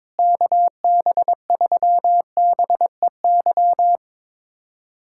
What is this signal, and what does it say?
Morse code, heard as a keyed tone, about 22 wpm, 700 Hz. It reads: KB3BEY
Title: USA Callsign DE Callsign - 6 Characters - 22wpm